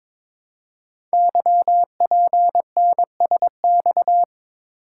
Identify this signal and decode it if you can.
Morse code, heard as a keyed tone, about 22 wpm, 690 Hz. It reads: YPNSX
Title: Code Groups - 5 Letter:Numbers - 22wpm